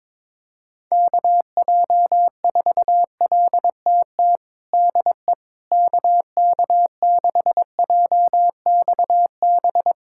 Morse code, heard as a keyed tone, about 22 wpm, 700 Hz. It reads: KJ4LTT DE KK6JXB
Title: USA Callsign DE Callsign 22wpm